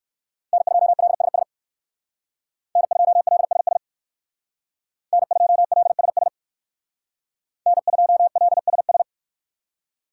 Morse code, heard as a keyed tone, about 58 wpm, 700 Hz. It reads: N2LSS N2LSS N2LSS N2LSS